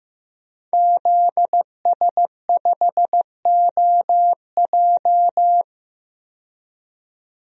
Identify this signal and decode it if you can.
Morse code, heard as a keyed tone, about 15 wpm, 700 Hz. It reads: ZS5OJ
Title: DXCC-100 Call Signs - 5 Characters - 15wpm